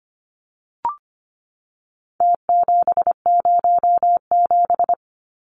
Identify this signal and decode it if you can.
Morse code, heard as a keyed tone, about 25 wpm, 700 Hz. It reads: T707
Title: Numbers - Contest Style - 4-Digit 25wpm